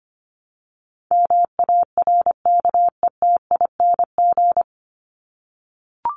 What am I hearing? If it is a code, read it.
Morse code, heard as a keyed tone, about 25 wpm, 700 Hz. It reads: MARKETING
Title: Top 751-1000 Words 25wpm